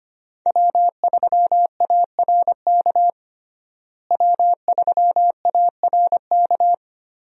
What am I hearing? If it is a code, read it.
Morse code, heard as a keyed tone, about 25 wpm, 700 Hz. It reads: W3ARK W3ARK
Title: USA Call Signs - 5 Characters - Repeated 2x - 25wpm